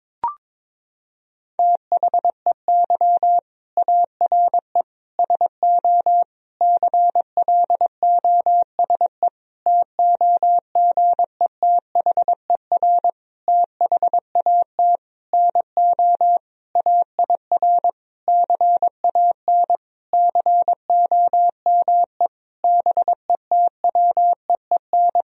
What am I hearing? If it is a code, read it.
Morse code, heard as a keyed tone, about 22 wpm, 700 Hz. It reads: THEY ARE SO CLOSE TOGETHER THAT NO AIR CAN COME BETWEEN
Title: Sentences from Top 1000 Words - Review Part 1 - 22wpm